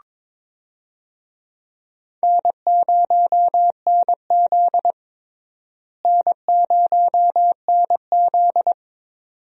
Morse code, heard as a keyed tone, about 22 wpm, 700 Hz. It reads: N0NZ N0NZ
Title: USA Call Signs - 4 Characters - Repeated 2x - 22wpm